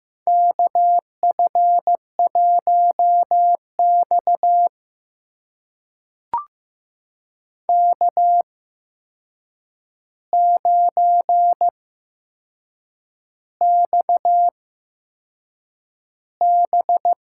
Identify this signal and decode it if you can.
Morse code, heard as a keyed tone, about 15 wpm, spaced as for 6 wpm, 700 Hz. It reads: KF1X K9XB
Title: USA Call Signs - 4 Characters - 8x Character Spacing - 15wpm